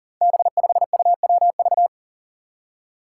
Morse code, heard as a keyed tone, about 40 wpm, 700 Hz. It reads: B5UWV
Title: Five Challenging Letter-Number Combinations 40wpm